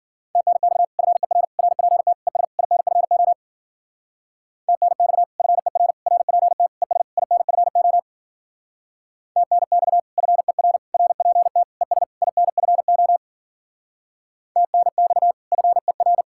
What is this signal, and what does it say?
Morse code, heard as a keyed tone, about 50 wpm, 700 Hz. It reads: TNX FER RPT ES INFO TNX FER RPT ES INFO TNX FER RPT ES INFO TNX FER
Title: Speed-Racing Common QSO Phrases Encore - 40wpm